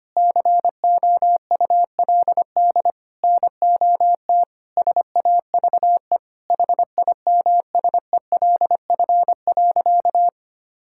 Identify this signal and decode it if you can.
Morse code, heard as a keyed tone, about 25 wpm, 700 Hz. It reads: COULD NOT SAVE HIMSELF.
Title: Sentences from Top 1600 Words 25wpm